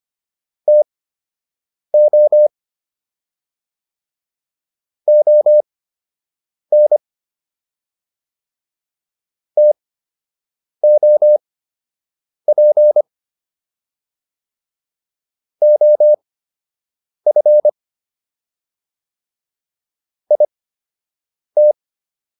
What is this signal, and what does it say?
Morse code, heard as a keyed tone, about 25 wpm, spaced as for 7 wpm, 600 Hz. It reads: TO ON TOP OF IT